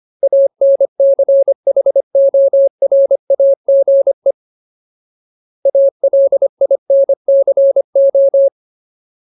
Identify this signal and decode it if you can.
Morse code, heard as a keyed tone, about 25 wpm, 550 Hz. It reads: ANCHORAGE ALINCO
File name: Ultimate Word List / 25WPM Ultimate